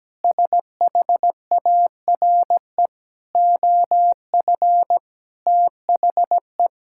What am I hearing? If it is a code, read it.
Morse code, heard as a keyed tone, about 17 wpm, 700 Hz. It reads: SHARE OF THE
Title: Sentences from Top 1000 Words - Review Part 6 - 17wpm